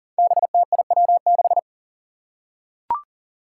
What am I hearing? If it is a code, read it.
Morse code, heard as a keyed tone, about 38 wpm, 700 Hz. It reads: BTIW6
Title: Five Challenging Letter-Number Combinations 40wpm